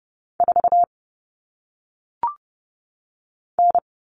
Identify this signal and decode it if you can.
Morse code, heard as a keyed tone, about 30 wpm, 700 Hz. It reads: 4 N
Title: All Cut Numbers - 1-Digit 30wpm